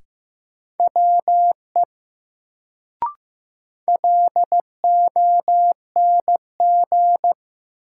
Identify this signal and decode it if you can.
Morse code, heard as a keyed tone, about 15 wpm, 700 Hz. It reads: WE LONG
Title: Sentences from Top 100 Words - New Word Drill - 15wpm